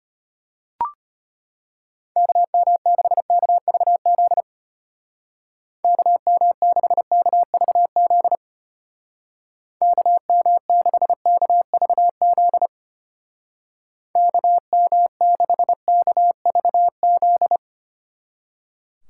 Morse code, heard as a keyed tone, about 31 wpm, 700 Hz. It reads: KM6KVZ KM6KVZ KM6KVZ KM6KVZ
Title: Speed Racing - USA 2x3 Call Signs 25wpm